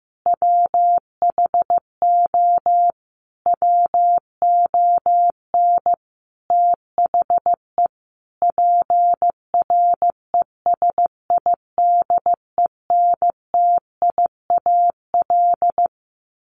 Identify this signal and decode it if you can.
Morse code, heard as a keyed tone, about 15 wpm, 700 Hz. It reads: WHO WON THE PRESIDENTIAL
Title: A Sound Of Thunder 15wpm